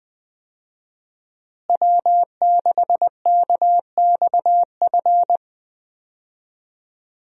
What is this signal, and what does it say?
Morse code, heard as a keyed tone, about 20 wpm, 700 Hz. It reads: W6KXF
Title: USA Call Signs - 5 Characters - Repeated 2x - 20wpm